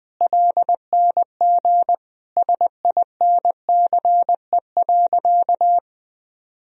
Morse code, heard as a keyed tone, about 21 wpm, 700 Hz. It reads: LNG SINCE.
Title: Sentences from Top 300 Words and Abbreviations 20wpm